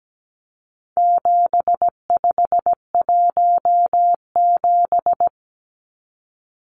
Morse code, heard as a keyed tone, about 17 wpm, 700 Hz. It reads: 7517
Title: Numbers - 4-Digit 17wpm